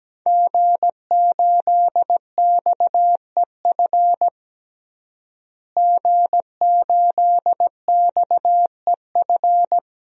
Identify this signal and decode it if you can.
Morse code, heard as a keyed tone, about 17 wpm, 700 Hz. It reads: G8XEF G8XEF
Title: DXCC-100 Call Signs - 5 Characters - Repeated 2x - 17wpm